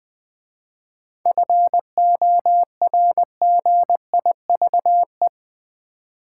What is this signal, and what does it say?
Morse code, heard as a keyed tone, about 20 wpm, 700 Hz. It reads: FORGIVE
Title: 7 Letter Words - First 3 Letters Given - 20wpm